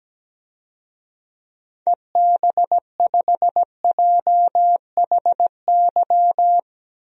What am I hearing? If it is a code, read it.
Morse code, heard as a keyed tone, about 17 wpm, 700 Hz. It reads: EB5JHY
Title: DXCC-100 Call Signs - 6 Characters - Repeated 3x - 17wpm